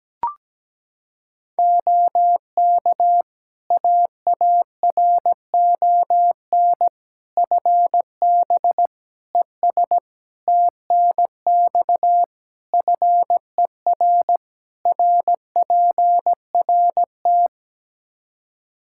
Morse code, heard as a keyed tone, about 17 wpm, 700 Hz. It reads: OK AARON FB ES TNX FER RPRT
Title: Practice QSO - Exchange 2 - Info - 17wpm